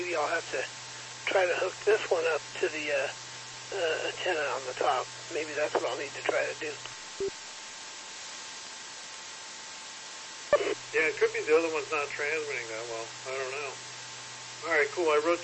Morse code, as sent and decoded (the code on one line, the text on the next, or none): none